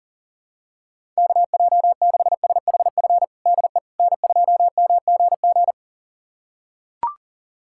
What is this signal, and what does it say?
Morse code, heard as a keyed tone, about 40 wpm, 700 Hz. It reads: KJ6SHF DE N2MGG
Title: USA Callsign DE Callsign 40wpm